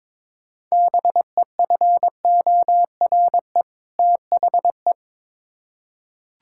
Morse code, heard as a keyed tone, about 22 wpm, 700 Hz. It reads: BEFORE THE
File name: Build A Phrase - 2-Word Phrases - 22wpm